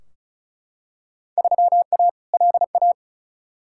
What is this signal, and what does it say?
Morse code, heard as a keyed tone, about 34 wpm, 700 Hz. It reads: 3A LA
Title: Field Day Exchanges - 35wpm